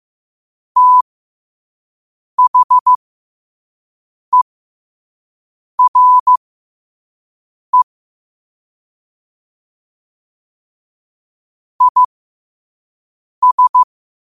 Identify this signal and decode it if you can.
Morse code, heard as a keyed tone, about 15 wpm, spaced as for 5 wpm, 1000 Hz. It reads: THERE IS
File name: qotdfiles / 5wpm